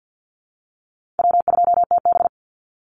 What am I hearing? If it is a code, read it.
Morse code, heard as a keyed tone, about 50 wpm, 700 Hz. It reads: W2TB
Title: Call Signs Repeated 2x - 50wpm